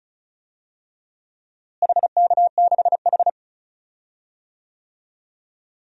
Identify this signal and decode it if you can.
Morse code, heard as a keyed tone, about 34 wpm, 700 Hz. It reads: HK6H
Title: DXCC-100 Call Signs - 4 Characters - Repeated 2x - 35wpm